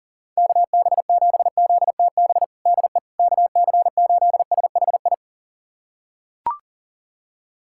Morse code, heard as a keyed tone, about 37 wpm, 700 Hz. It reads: KB7ZTB DE KC8SHI
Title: USA Callsign DE Callsign - 6 Characters - 40wpm